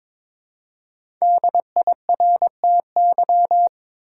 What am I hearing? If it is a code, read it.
Morse code, heard as a keyed tone, about 22 wpm, 700 Hz. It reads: DIRTY
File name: 5 Letter Words - First 3 Letters Given - 22wpm